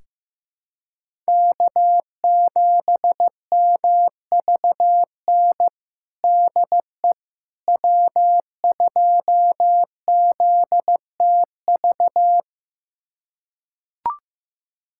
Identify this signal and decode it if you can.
Morse code, heard as a keyed tone, about 15 wpm, 700 Hz. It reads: K7MVN DE W2ZTV
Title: USA Callsign DE Callsign 15wpm